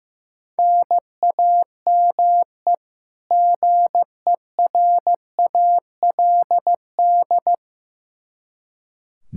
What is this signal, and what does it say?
Morse code, heard as a keyed tone, about 15 wpm, 700 Hz. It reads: NAME GERALD
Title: Practice QSO - Exchange 1 - Beginning - 15wpm